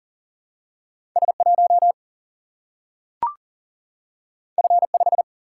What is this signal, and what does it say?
Morse code, heard as a keyed tone, about 40 wpm, 700 Hz. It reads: S 1 F 5
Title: Single-Letter Single-Number 40wpm